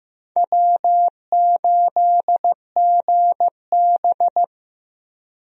Morse code, heard as a keyed tone, about 16 wpm, 700 Hz. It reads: W8GB
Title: USA Call Signs - 4 Characters - Repeated 2x - 15wpm